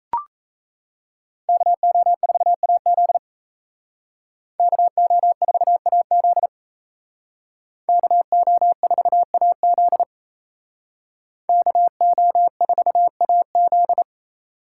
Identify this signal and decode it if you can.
Morse code, heard as a keyed tone, about 36 wpm, 700 Hz. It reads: KO4AZ KO4AZ KO4AZ KO4AZ